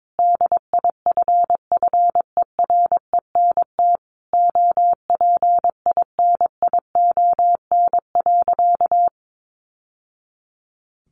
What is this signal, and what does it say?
Morse code, heard as a keyed tone, about 22 wpm, 700 Hz. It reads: DIFFERENT OPINION.
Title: Sentences from Top 1200 Words 22wpm